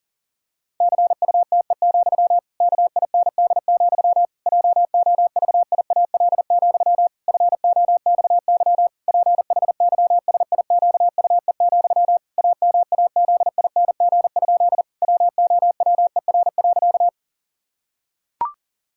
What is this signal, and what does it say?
Morse code, heard as a keyed tone, about 40 wpm, 700 Hz. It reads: CUTE, KIND, JOVIAL, FOXY PHYSIQUE, AMAZING? WOWER.
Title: Pangram Challenge v2 - 40wpm